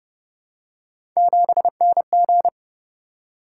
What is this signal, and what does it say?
Morse code, heard as a keyed tone, about 28 wpm, 700 Hz. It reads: 7NG